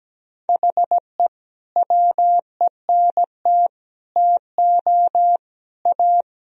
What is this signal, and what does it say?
Morse code, heard as a keyed tone, about 17 wpm, 700 Hz. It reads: HE WENT TO A
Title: Sentences from Top 1000 Words - Review Part 1 - 17wpm